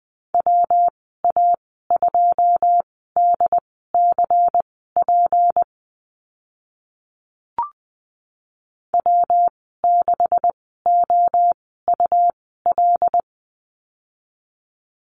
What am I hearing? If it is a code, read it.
Morse code, heard as a keyed tone, about 20 wpm, spaced as for 14 wpm, 700 Hz. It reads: WA2DCP W6OUL
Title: 179 Lesson 29g - Slash - Call Signs 20wpm